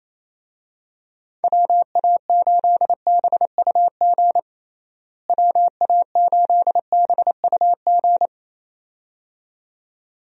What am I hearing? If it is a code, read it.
Morse code, heard as a keyed tone, about 27 wpm, 700 Hz. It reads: WA8BUG WA8BUG
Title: USA Call Signs - 6 Characters - Repeated 2x - 28wpm